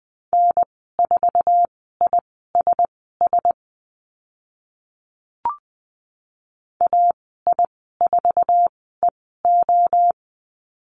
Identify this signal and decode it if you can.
Morse code, heard as a keyed tone, about 20 wpm, spaced as for 14 wpm, 700 Hz. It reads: N4ISS AI4EO
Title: 062 Lesson 11g - Letter h - Call Signs 20wpm